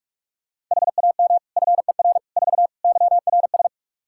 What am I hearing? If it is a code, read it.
Morse code, heard as a keyed tone, about 46 wpm, 700 Hz. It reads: HAM FER 4 YRS
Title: Practice QSO - Exchange 3 - More Info - 45wpm